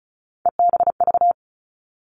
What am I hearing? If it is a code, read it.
Morse code, heard as a keyed tone, about 36 wpm, 700 Hz. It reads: EBV
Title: Three Challenging Letter-Number Combinations 35wpm